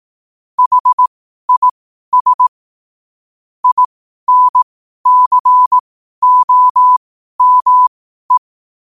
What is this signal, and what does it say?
Morse code, heard as a keyed tone, about 18 wpm, spaced as for 12 wpm, 1000 Hz. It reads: HIS INCOME
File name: qotdfiles / 12wpm